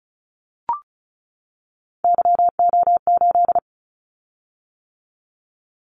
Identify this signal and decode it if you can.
Morse code, heard as a keyed tone, about 36 wpm, 700 Hz. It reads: YO8
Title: Callsign Prefixes for Top 25 DXCC - 35wpm